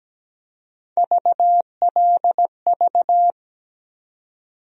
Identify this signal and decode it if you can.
Morse code, heard as a keyed tone, about 17 wpm, 700 Hz. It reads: VLV
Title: USA Call Sign Postfixes - 3 Letters 17wpm